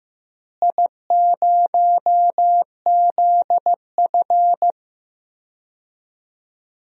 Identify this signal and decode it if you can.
Morse code, heard as a keyed tone, about 15 wpm, 700 Hz. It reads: I0ZF
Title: DXCC-100 Call Signs - 4 Characters - Repeated 2x - 15wpm